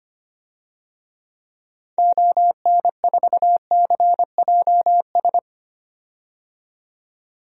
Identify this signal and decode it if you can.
Morse code, heard as a keyed tone, about 25 wpm, 700 Hz. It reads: ON4CJS
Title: DXCC-100 Call Signs - 6 Characters - 25wpm